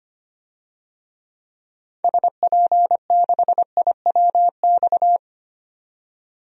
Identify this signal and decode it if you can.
Morse code, heard as a keyed tone, about 25 wpm, 700 Hz. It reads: SP6IWX
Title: DXCC-100 Call Signs - 6 Characters - 25wpm